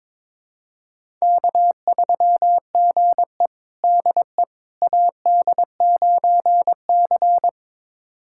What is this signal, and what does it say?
Morse code, heard as a keyed tone, about 22 wpm, 700 Hz. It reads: K3GE DE AD9C
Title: USA Callsign DE Callsign - 4 Characters - 22wpm